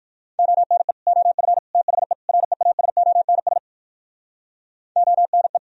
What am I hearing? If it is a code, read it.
Morse code, heard as a keyed tone, about 67 wpm, 700 Hz. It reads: ONE OF THE REASONS ONE